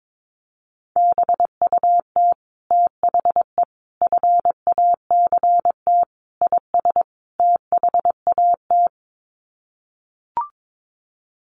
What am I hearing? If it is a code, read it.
Morse code, heard as a keyed tone, about 22 wpm, 700 Hz. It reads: BUT THE FACT IS THAT
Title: Sets of 5 Words Encore2 22wpm